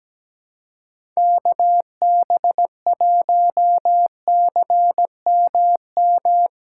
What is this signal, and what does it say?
Morse code, heard as a keyed tone, about 17 wpm, 700 Hz. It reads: KB1CMM
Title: USA Callsign DE Callsign - 6 Characters - 17wpm